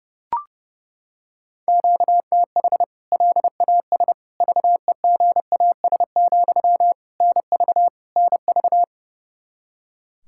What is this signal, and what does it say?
Morse code, heard as a keyed tone, about 30 wpm, 700 Hz. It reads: QTH LAS VEGAS, NV NV
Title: K1USN SST QSO Practice 30wpm